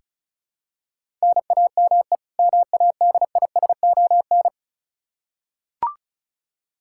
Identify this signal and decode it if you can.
Morse code, heard as a keyed tone, about 36 wpm, 700 Hz. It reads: NAME MADISON
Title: Practice QSO - Exchange 1 - Beginning - 35wpm